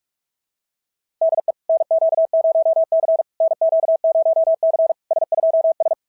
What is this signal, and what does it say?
Morse code, heard as a keyed tone, about 43 wpm, 650 Hz. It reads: DE NQ0C NQ0C S2S
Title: SOTA QSO Practice 45wpm